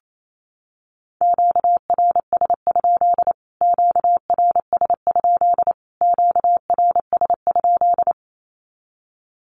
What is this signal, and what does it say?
Morse code, heard as a keyed tone, about 28 wpm, 700 Hz. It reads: QRS? QRS? QRS?
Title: Q-Codes Repeated 3x 28wpm